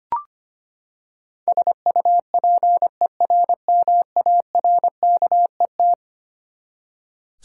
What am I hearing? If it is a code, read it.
Morse code, heard as a keyed tone, about 25 wpm, 700 Hz. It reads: SUPERMARKET
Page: morse-code-ninja-practice-25wpm